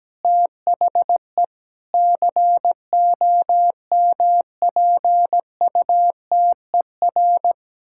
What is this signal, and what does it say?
Morse code, heard as a keyed tone, about 17 wpm, 700 Hz. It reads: THE COMPUTER
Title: Sentences from Top 1000 Words - Review Part 6 - 17wpm